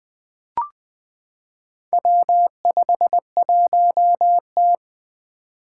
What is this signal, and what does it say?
Morse code, heard as a keyed tone, about 20 wpm, 700 Hz. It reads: W51T
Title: Four Challenging Letter-Number Combinations 20wpm